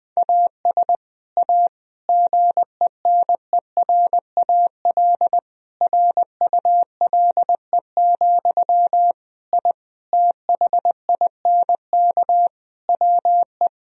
Morse code, heard as a keyed tone, about 20 wpm, 700 Hz. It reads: AS A GENERAL RULE, I THINK WE
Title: Sentences from Top 1000 Words - Review Part 3 - 20wpm